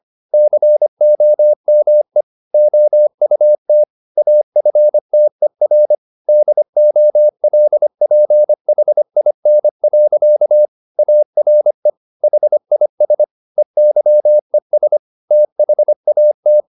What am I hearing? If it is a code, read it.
Morse code, heard as a keyed tone, about 25 wpm, 600 Hz. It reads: COME OUT AFTER DOLPHIN. ARE HIS EYES THAT
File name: the old man and the sea part 1 25wpm